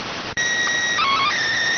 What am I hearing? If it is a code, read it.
SSTV (slow-scan television), PD 290.